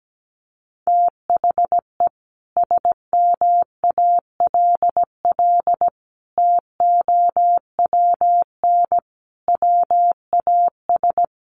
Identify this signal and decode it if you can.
Morse code, heard as a keyed tone, about 17 wpm, 700 Hz. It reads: THE SMALL TOWN WAS
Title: Sentences from Top 1300 Words 17wpm